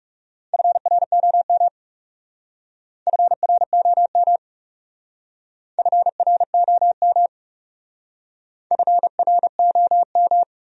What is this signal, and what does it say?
Morse code, heard as a keyed tone, about 38 wpm, 700 Hz. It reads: FROM FROM FROM FROM